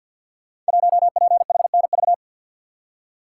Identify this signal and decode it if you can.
Morse code, heard as a keyed tone, about 49 wpm, 700 Hz. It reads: 1PHNV
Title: Code Groups - 5 Letter:Numbers - 50wpm